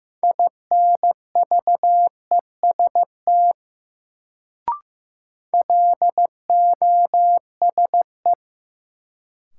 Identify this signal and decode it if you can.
Morse code, heard as a keyed tone, about 15 wpm, 700 Hz. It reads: INVEST LOSE
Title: Sentences from Top 1100 Words - New Word Drill Alt - 15wpm